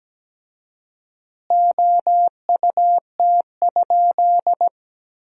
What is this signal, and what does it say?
Morse code, heard as a keyed tone, about 17 wpm, 700 Hz. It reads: OUT?
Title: 4x Word-Spacing - Sentences from Top 100 Words - 17wpm